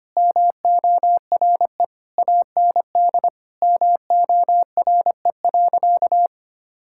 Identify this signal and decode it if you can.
Morse code, heard as a keyed tone, about 25 wpm, 700 Hz. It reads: MORE AND MORE.
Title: Sentences from Top 1000 Words - Review Part 5 - 25wpm